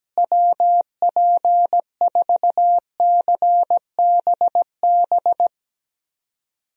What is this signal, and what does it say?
Morse code, heard as a keyed tone, about 17 wpm, 700 Hz. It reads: WP4CBB
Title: USA Call Signs - 6 Characters - Repeated 3x - 17wpm